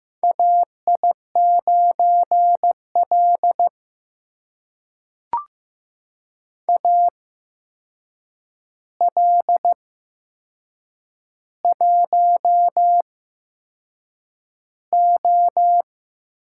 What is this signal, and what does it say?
Morse code, heard as a keyed tone, about 15 wpm, spaced as for 6 wpm, 700 Hz. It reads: AI9L AL1O
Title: USA Call Signs - 4 Characters - 8x Character Spacing - 15wpm